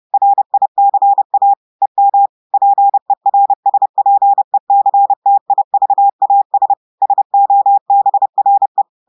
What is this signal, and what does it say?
Morse code, heard as a keyed tone, about 30 wpm, 800 Hz. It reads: RICA EM PERSPECTIVAS SOBRE
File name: audios / portugais / portugais texte en clair 1 vitesse 30 mots minutes